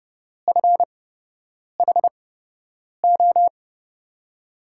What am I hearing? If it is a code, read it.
Morse code, heard as a keyed tone, about 30 wpm, spaced as for 8 wpm, 700 Hz. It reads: FHO
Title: USA Call Signs - 5 Characters - 8x Character Spacing - 30wpm